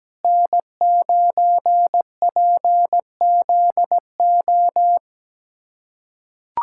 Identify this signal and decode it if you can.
Morse code, heard as a keyed tone, about 17 wpm, 700 Hz. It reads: N9PZO
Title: USA Callsign DE Callsign - 5 Characters - 17wpm